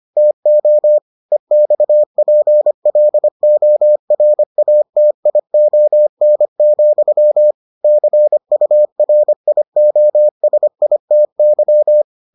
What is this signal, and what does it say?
Morse code, heard as a keyed tone, about 25 wpm, 600 Hz. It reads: TO EXPLORATION, CURIOSITY